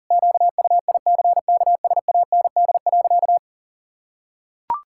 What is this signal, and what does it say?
Morse code, heard as a keyed tone, about 40 wpm, 700 Hz. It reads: QUICKSAND.
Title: Pangram Challenge v2 - 40wpm